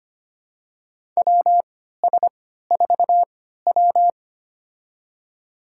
Morse code, heard as a keyed tone, about 25 wpm, spaced as for 14 wpm, 700 Hz. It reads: WS4W
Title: USA Call Signs - 4 Characters - 3x Character Spacing - 25wpm